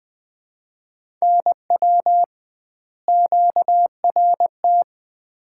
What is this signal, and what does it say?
Morse code, heard as a keyed tone, about 20 wpm, 700 Hz. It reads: NW QRT
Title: Common QSO Phrases - 2x Word-Spacing - 20wpm